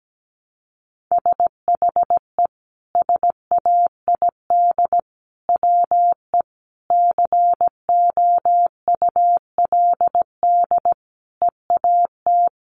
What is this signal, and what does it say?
Morse code, heard as a keyed tone, about 17 wpm, 700 Hz. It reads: SHE SAID WE COULD EAT
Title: Sentences from Top 1300 Words 17wpm